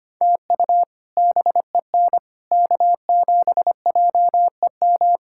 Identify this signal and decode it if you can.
Morse code, heard as a keyed tone, about 25 wpm, 700 Hz. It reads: TU BEN K7JEM
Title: K1USN SST QSO Practice - 25wpm